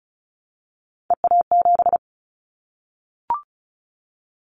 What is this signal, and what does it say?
Morse code, heard as a keyed tone, about 34 wpm, 700 Hz. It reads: EA7
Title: Callsign Prefixes for Top 25 DXCC - Repeated 3x - 35wpm